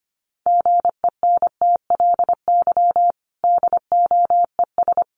Morse code, heard as a keyed tone, about 25 wpm, 700 Hz. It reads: GENTLY DOES
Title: Sentences from Top 1500 Words 25wpm